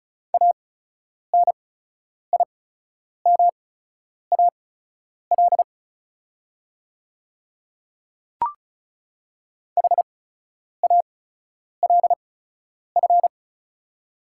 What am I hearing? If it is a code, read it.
Morse code, heard as a keyed tone, about 36 wpm, spaced as for 10 wpm, 690 Hz. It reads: ANIMAL HALF
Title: 8x Character Spacing - Top 500 Words - 35wpm